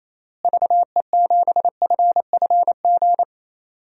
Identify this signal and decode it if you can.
Morse code, heard as a keyed tone, about 27 wpm, 700 Hz. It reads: VE7FFG
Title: DXCC-100 Call Signs - 6 Characters - Repeated 2x - 28wpm